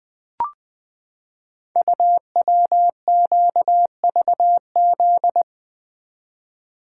Morse code, heard as a keyed tone, about 20 wpm, 700 Hz. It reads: UWQVZ
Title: Code Groups - 5 Letters - 20wpm